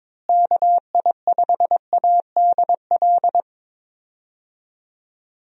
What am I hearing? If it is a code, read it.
Morse code, heard as a keyed tone, about 22 wpm, 700 Hz. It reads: KI5ADL
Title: USA Call Signs - 6 Characters - Repeated 3x - 22wpm